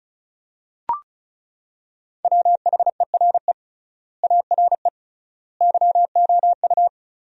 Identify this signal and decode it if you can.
Morse code, heard as a keyed tone, about 36 wpm, 700 Hz. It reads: WHERE ARE YOU
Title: 3x Word-Spacing - Sentences from Top 200 Words - 35wpm